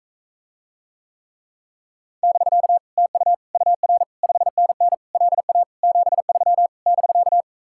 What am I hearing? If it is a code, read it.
Morse code, heard as a keyed tone, about 45 wpm, 700 Hz. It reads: <BK> TU UR 5NN LA 73 <BK>